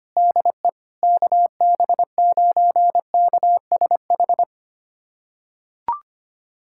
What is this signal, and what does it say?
Morse code, heard as a keyed tone, about 25 wpm, 700 Hz. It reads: DE KB9KSH
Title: USA Callsign DE Callsign - 6 Characters - 25wpm